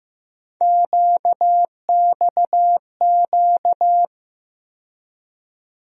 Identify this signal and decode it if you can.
Morse code, heard as a keyed tone, about 15 wpm, 700 Hz. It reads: QXQ
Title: USA Call Sign Postfixes - 3 Letters 15wpm